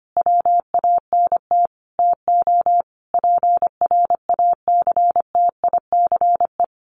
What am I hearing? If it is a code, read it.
Morse code, heard as a keyed tone, about 25 wpm, 700 Hz. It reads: WANT TO PRACTICE
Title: Sentences from Top 700 Words 25wpm